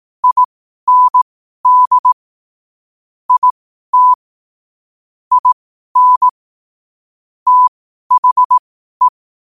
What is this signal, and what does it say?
Morse code, heard as a keyed tone, about 18 wpm, spaced as for 12 wpm, 1000 Hz. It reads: IND IT IN THE
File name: qotdfiles / 12wpm